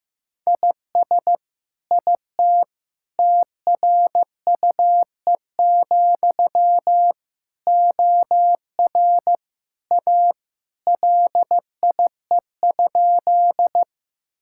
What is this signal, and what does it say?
Morse code, heard as a keyed tone, about 16 wpm, 700 Hz. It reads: IS IT TRUE, OR A LIE?
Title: Sentences from Top 1600 Words 15wpm